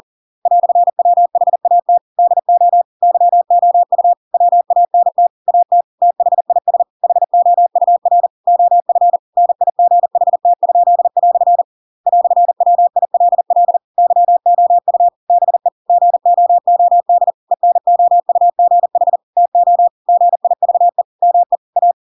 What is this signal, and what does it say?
Morse code, heard as a keyed tone, about 40 wpm, 700 Hz. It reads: "WHAT DO YOU WANT AT THIS HOUR OF NIGHT?" "WILL YOU BE GOOD ENOUGH TO GIVE ME A
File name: Pinocchio / Pinocchio ch6 40 40 ik1hge